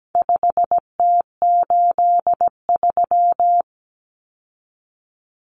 Morse code, heard as a keyed tone, about 17 wpm, 700 Hz. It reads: 5T83
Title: Common Cut Numbers - 4-Digit 17wpm